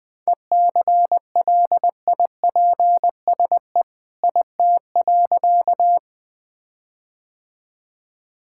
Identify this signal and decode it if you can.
Morse code, heard as a keyed tone, about 20 wpm, 700 Hz. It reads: ECLIPSE IT.
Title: All-Age Appropriate Jokes - 20wpm